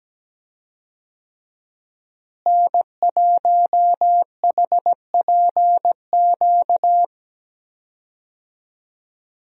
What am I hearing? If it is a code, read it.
Morse code, heard as a keyed tone, about 17 wpm, 700 Hz. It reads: N1HPQ
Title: USA Call Signs - 5 Characters - Repeated 2x - 17wpm